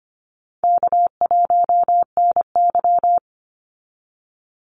Morse code, heard as a keyed tone, about 25 wpm, 700 Hz. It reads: K1NY
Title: Easy Call Signs Repeated 3x - 25wpm